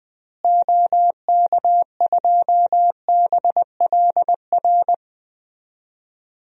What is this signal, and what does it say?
Morse code, heard as a keyed tone, about 20 wpm, 700 Hz. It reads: OK2BLR
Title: DXCC-100 Call Signs - 6 Characters - Repeated 3x - 20wpm